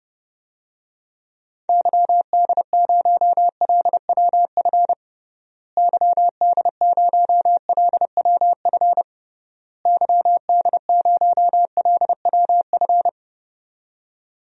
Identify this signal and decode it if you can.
Morse code, heard as a keyed tone, about 30 wpm, 700 Hz. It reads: YD0LWF YD0LWF YD0LWF
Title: DXCC-100 Call Signs - 6 Characters - Repeated 3x - 30wpm